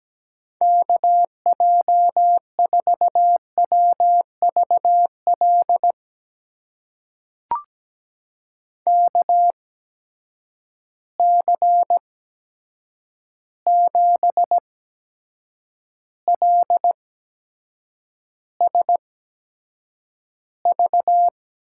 Morse code, heard as a keyed tone, about 17 wpm, spaced as for 7 wpm, 700 Hz. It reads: KJ4WVL KC7LSV
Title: USA Call Signs - 6 Characters - 8x Character Spacing - 17wpm